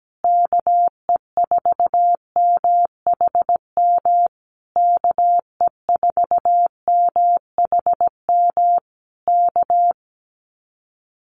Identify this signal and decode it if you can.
Morse code, heard as a keyed tone, about 17 wpm, 700 Hz. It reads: KE4MHM KE4MHM K
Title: Common QSO Phrases Encore 17wpm